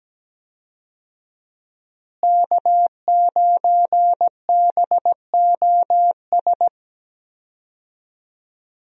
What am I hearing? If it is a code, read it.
Morse code, heard as a keyed tone, about 17 wpm, 700 Hz. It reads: K9BOS
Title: USA Call Signs - 5 Characters - Repeated 3x - 17wpm